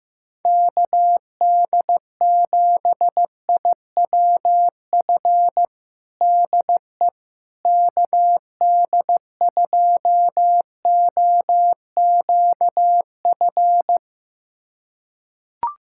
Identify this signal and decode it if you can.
Morse code, heard as a keyed tone, about 16 wpm, 700 Hz. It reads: KD7IWF DE KD2OQF
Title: USA Callsign DE Callsign - 6 Characters - 15wpm